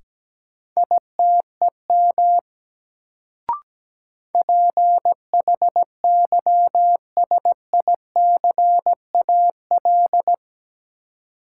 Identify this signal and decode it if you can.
Morse code, heard as a keyed tone, about 17 wpm, 700 Hz. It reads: ITEM PHYSICAL
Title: Sentences from Top 1400 Words - New Word Drill Alt - 17wpm